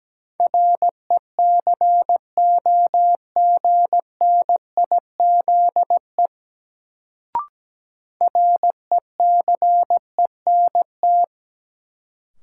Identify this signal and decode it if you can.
Morse code, heard as a keyed tone, about 17 wpm, 700 Hz. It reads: RECOGNIZE RECENT
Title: Sentences from Top 1400 Words - New Word Drill Alt - 17wpm